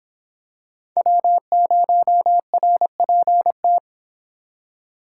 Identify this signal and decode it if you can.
Morse code, heard as a keyed tone, about 26 wpm, 700 Hz. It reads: W0RPT